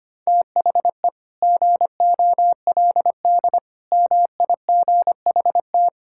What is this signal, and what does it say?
Morse code, heard as a keyed tone, about 25 wpm, 700 Hz. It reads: THE GOLD MIGHT
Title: Sentences from Top 1000 Words - Review Part 5 - 25wpm